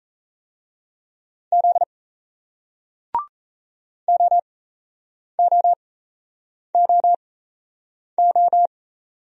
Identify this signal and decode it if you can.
Morse code, heard as a keyed tone, about 38 wpm, spaced as for 8 wpm, 700 Hz. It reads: Z OOOO